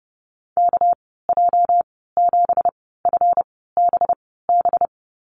Morse code, heard as a keyed tone, about 30 wpm, spaced as for 17 wpm, 700 Hz. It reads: KJ7FBB
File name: Lesson 37g - Letter x - Call Signs 30wpm